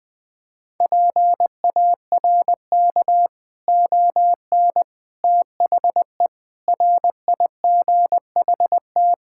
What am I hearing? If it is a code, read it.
Morse code, heard as a keyed tone, about 21 wpm, 700 Hz. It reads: PARK ON THE RIGHT
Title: Sentences with 6-Word Phrases - Spoken Beginning - 20wpm